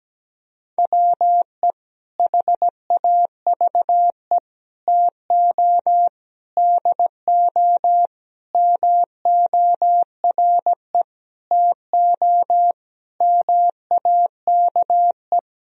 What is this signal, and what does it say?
Morse code, heard as a keyed tone, about 17 wpm, 700 Hz. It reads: WE HAVE TO DO MORE TO MAKE
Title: Sentences from Top 1000 Words - Review Part 3 - 17wpm